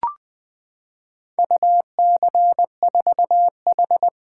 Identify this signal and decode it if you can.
Morse code, heard as a keyed tone, about 20 wpm, 700 Hz. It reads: UC4H
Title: Four Letter-Number Combinations - 20wpm